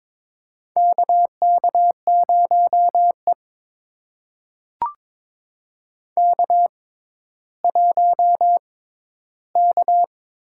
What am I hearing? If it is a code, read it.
Morse code, heard as a keyed tone, about 22 wpm, spaced as for 12 wpm, 700 Hz. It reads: KK0E K1K
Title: USA Call Signs - 6x Character Spacing - 22wpm